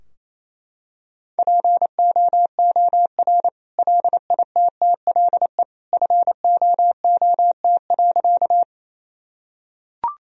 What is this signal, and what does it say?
Morse code, heard as a keyed tone, about 28 wpm, 700 Hz. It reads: POOR LITTLE FOOT.
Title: Sentences from Top 1000 Words - Review Part 3 - 28wpm